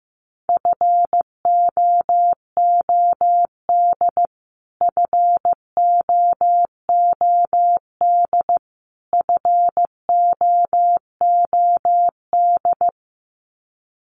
Morse code, heard as a keyed tone, about 15 wpm, 700 Hz. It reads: FOOD FOOD FOOD
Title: Top 500 Words Repeated 15wpm